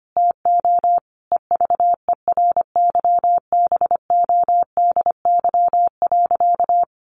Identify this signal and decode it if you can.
Morse code, heard as a keyed tone, about 25 wpm, 700 Hz. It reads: TO EVERYBODY.
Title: Sentences from Top 1400 Words 25wpm